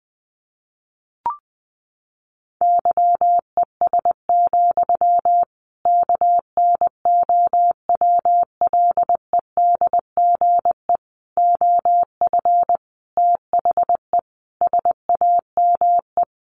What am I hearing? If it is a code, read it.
Morse code, heard as a keyed tone, about 20 wpm, 700 Hz. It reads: YES, KNOWLEDGE OF THE SAME
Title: Sentences from Top 1500 Words 20wpm